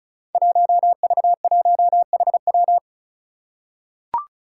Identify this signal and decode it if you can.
Morse code, heard as a keyed tone, about 36 wpm, 700 Hz. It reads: 1V1HW
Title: Five Challenging Letter-Number Combinations 35wpm